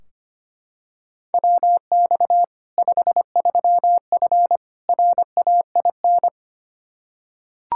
Morse code, heard as a keyed tone, about 25 wpm, 700 Hz. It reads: WX 53F RAIN
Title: Practice QSO - Exchange 2 - Info - 25wpm